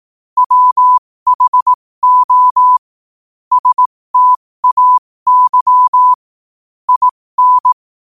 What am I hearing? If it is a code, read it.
Morse code, heard as a keyed tone, about 18 wpm, spaced as for 15 wpm, 1000 Hz. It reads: WHO STAY IN
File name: qotdfiles / 15wpm